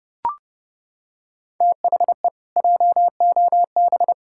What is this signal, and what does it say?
Morse code, heard as a keyed tone, about 32 wpm, 700 Hz. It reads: THE JOB
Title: Pangram Challenge v2 - 30wpm